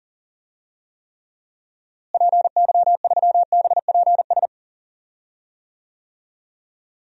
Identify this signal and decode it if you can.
Morse code, heard as a keyed tone, about 40 wpm, 700 Hz. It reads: PY3BPS
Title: DXCC-100 Call Signs - 6 Characters - Repeated 2x - 40wpm